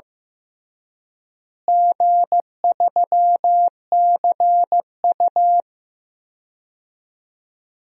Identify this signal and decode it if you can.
Morse code, heard as a keyed tone, about 16 wpm, 700 Hz. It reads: G3CU
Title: DXCC-100 Call Signs - 4 Characters - Repeated 3x - 15wpm